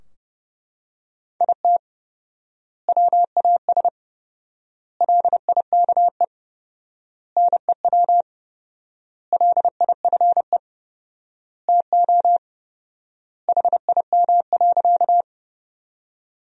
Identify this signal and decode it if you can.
Morse code, heard as a keyed tone, about 30 wpm, 700 Hz. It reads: IT WAS LIKE NEW LIFE TO HIM.
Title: 4x Word-Spacing - Sentences from Top 300 Words - 30wpm